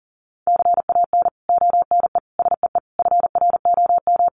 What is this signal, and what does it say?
Morse code, heard as a keyed tone, about 38 wpm, 700 Hz. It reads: CAN ONE SEE FROM
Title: Sentences from Top 100 Words 40wpm